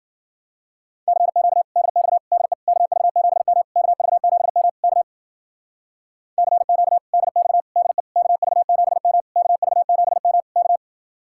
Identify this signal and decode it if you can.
Morse code, heard as a keyed tone, about 50 wpm, 700 Hz. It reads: CQ DX DE KV7M KV7M K CQ DX DE KV7M KV7M K